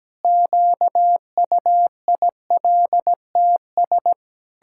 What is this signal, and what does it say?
Morse code, heard as a keyed tone, about 17 wpm, 700 Hz. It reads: QUILTS
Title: Pangram Challenge v2 - 17wpm